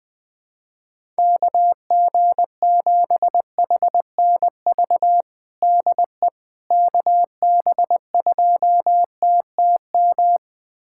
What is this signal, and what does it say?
Morse code, heard as a keyed tone, about 20 wpm, 700 Hz. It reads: KG7HNV DE KB2TTM
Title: USA Callsign DE Callsign - 6 Characters - 20wpm